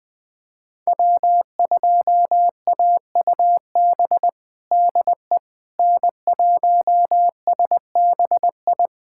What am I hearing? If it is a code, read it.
Morse code, heard as a keyed tone, about 21 wpm, 700 Hz. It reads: W2AUB DE N1SBI
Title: USA Callsign DE Callsign 20wpm